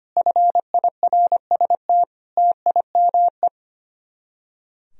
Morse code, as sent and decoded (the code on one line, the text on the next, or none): ..-. .. .-. ... - / - .. -- .
FIRST TIME